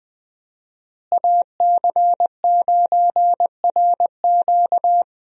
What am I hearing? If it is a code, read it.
Morse code, heard as a keyed tone, about 20 wpm, 700 Hz. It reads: AC9RQ
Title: USA Call Signs - 5 Characters - Repeated 2x - 20wpm